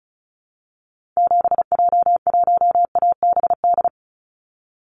Morse code, heard as a keyed tone, about 34 wpm, 700 Hz. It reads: 7J1ABD
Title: Call Signs Repeated 3x - 35wpm